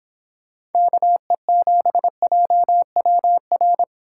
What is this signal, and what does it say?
Morse code, heard as a keyed tone, about 29 wpm, 700 Hz. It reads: KE7JWR